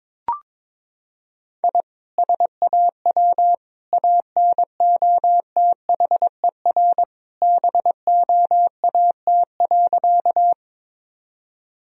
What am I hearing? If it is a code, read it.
Morse code, heard as a keyed tone, about 22 wpm, 700 Hz. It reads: I SAW ANOTHER BOAT.
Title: Sentences from Top 1000 Words - Review Part 3 - 22wpm